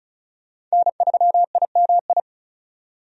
Morse code, heard as a keyed tone, about 34 wpm, 700 Hz. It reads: N3IMI
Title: USA Call Signs - 5 Characters - 35wpm